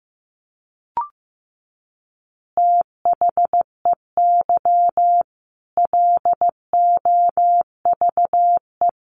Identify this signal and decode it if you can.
Morse code, heard as a keyed tone, about 15 wpm, 700 Hz. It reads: THEY LOVE
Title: Easy Phrases 15wpm